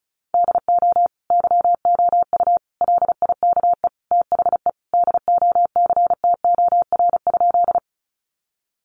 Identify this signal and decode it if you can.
Morse code, heard as a keyed tone, about 34 wpm, 700 Hz. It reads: DO YOU LIKE THE DOCTOR?
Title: Sentences from Top 1500 Words 35wpm